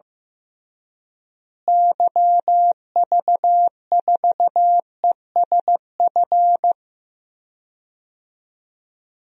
Morse code, heard as a keyed tone, about 15 wpm, 700 Hz. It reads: YV4ESF
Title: DXCC-100 Call Signs - 6 Characters - Repeated 3x - 15wpm